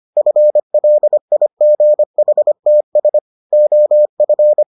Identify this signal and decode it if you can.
Morse code, heard as a keyed tone, about 25 wpm, 600 Hz. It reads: FLIGHTS OF